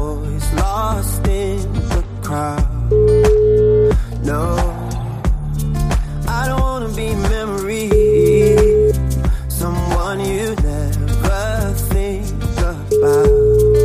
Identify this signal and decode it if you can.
Ringback tone (Europe, 425 Hz), with noise.